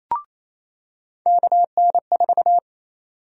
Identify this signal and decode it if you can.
Morse code, heard as a keyed tone, about 28 wpm, 700 Hz. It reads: KN4
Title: USA Call Sign Prefixes - 28wpm